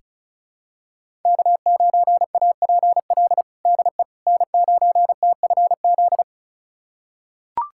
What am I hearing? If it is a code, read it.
Morse code, heard as a keyed tone, about 36 wpm, 700 Hz. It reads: K9APL DE N9TFZ
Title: USA Callsign DE Callsign - 5 Characters - 35wpm